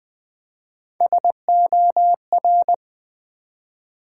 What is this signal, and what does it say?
Morse code, heard as a keyed tone, about 20 wpm, 700 Hz. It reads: SOR
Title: Build A Word - 1 Letter At A Time - 3 to 5-Letter Words - 20wpm